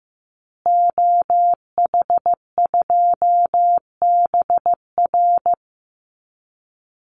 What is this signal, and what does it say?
Morse code, heard as a keyed tone, about 15 wpm, 700 Hz. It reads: OH2BR
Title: Call Signs Repeated 3x - 15wpm